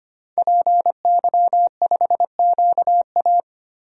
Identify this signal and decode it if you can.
Morse code, heard as a keyed tone, about 25 wpm, 700 Hz. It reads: PY5QA
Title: DXCC-100 Call Signs - 5 Characters - Repeated 3x - 25wpm